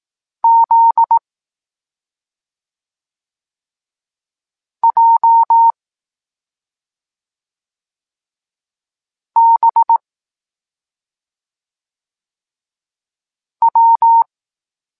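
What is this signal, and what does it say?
Morse code, heard as a keyed tone, about 18 wpm, 920 Hz. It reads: Z J B W
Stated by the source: Evenly spaced Morse lesson text 21 for learning letters G and Z, from SP9MOA Niepołomicki Klub Krótkofalowców.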